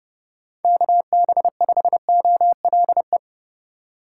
Morse code, heard as a keyed tone, about 30 wpm, 700 Hz. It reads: KB5OLE